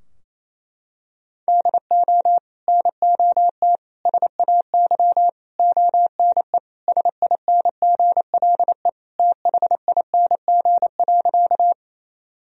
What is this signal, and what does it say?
Morse code, heard as a keyed tone, about 28 wpm, 700 Hz. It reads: DO NOT SAY ONE SINGLE THING.
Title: Sentences from Top 1000 Words - Review Part 4 - 28wpm